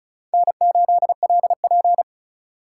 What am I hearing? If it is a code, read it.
Morse code, heard as a keyed tone, about 43 wpm, 700 Hz. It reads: N8LP
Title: Speed Racing - USA 1x2 Call Signs 35wpm